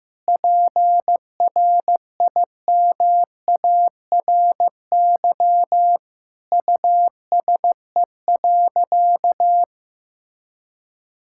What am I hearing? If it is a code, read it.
Morse code, heard as a keyed tone, about 15 wpm, 700 Hz. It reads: PRIMARY USE.
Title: Sentences from Top 1000 Words - Review Part 5 - 15wpm